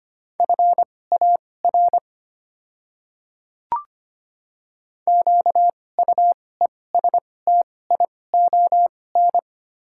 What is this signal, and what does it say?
Morse code, heard as a keyed tone, about 25 wpm, spaced as for 18 wpm, 700 Hz. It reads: FAR QUESTION
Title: 2x Character Spacing - Top 500 Words - 25wpm